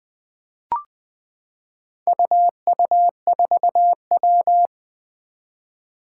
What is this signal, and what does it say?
Morse code, heard as a keyed tone, about 20 wpm, 700 Hz. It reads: UU4W
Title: Four Challenging Letter-Number Combinations 20wpm